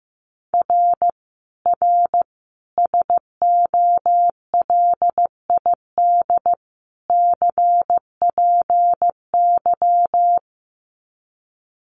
Morse code, heard as a keyed tone, about 15 wpm, 700 Hz. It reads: R R SOLID CPY
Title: Common QSO Phrases 15wpm